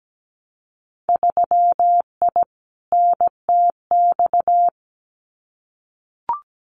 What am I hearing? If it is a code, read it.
Morse code, heard as a keyed tone, about 17 wpm, 700 Hz. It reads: 3I NTX
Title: Winter Field Day Exchanges 17wpm